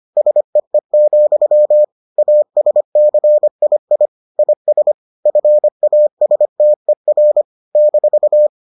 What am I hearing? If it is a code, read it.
Morse code, heard as a keyed tone, about 25 wpm, 600 Hz. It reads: SEE, ASCII IS FASTER -